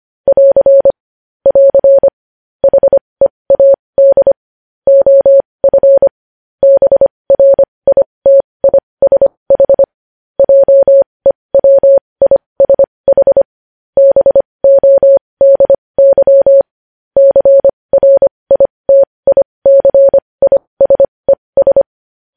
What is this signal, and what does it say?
Morse code, heard as a keyed tone, about 25 wpm, spaced as for 20 wpm, 550 Hz. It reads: <AR> <AR> HEAD OF BRITISH JEWISH BODY CRITICISES
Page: News Headlines in Morse Code at 20 WPM